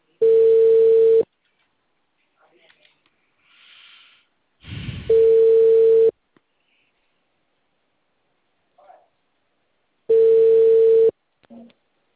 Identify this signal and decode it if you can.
Ringback tone (China, 450 Hz).
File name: out- -2012- - - .9918